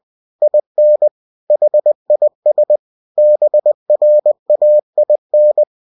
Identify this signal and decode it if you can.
Morse code, heard as a keyed tone, about 20 wpm, 600 Hz. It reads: IN HIS BRAIN
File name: cw-books / winnie-the-pooh / winnie-20wpm-page-0040